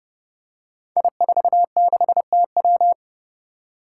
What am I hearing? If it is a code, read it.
Morse code, heard as a keyed tone, about 30 wpm, 700 Hz. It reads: I46TW
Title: Five Challenging Letter-Number Combinations 30wpm